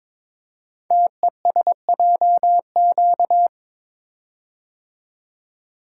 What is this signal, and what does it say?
Morse code, heard as a keyed tone, about 22 wpm, 700 Hz. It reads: TESJQ
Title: Code Groups - 5 Letters - 22wpm